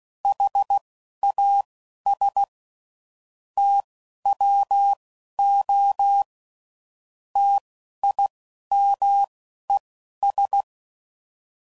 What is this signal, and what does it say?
Morse code, heard as a keyed tone, about 16 wpm, spaced as for 11 wpm, 800 Hz. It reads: HAS TWO TIMES
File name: CW Code Groups @10WPM / Technical Subjects / XC